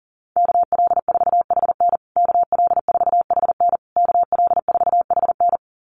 Morse code, heard as a keyed tone, about 37 wpm, 700 Hz. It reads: KL4HN KL4HN KL4HN
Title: USA Call Signs Repeated 3x 40wpm